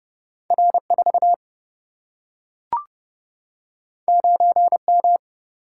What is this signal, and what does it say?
Morse code, heard as a keyed tone, about 30 wpm, 700 Hz. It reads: R 4 9 M
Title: Single-Letter Single-Number 30wpm